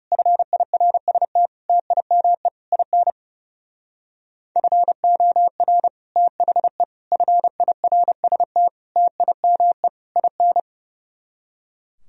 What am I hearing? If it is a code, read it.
Morse code, heard as a keyed tone, about 35 wpm, 700 Hz. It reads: FIRST TIME IN FOR THE FIRST TIME IN
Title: Speed-Racing - Sets of 5 Words 30wpm